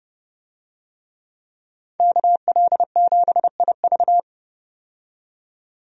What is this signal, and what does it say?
Morse code, heard as a keyed tone, about 30 wpm, 700 Hz. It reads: KL7IV
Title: DXCC-100 Call Signs - 5 Characters - 30wpm